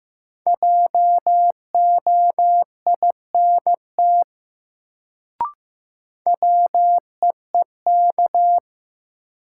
Morse code, heard as a keyed tone, about 15 wpm, 700 Hz. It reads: JOINT WEEK
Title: Sets of 5 Words Encore - New Word Drill - 15wpm